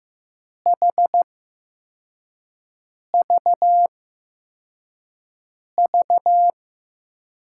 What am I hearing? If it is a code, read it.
Morse code, heard as a keyed tone, about 15 wpm, spaced as for 4 wpm, 700 Hz. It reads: HVV
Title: USA Call Signs - 8x Character Spacing - 15wpm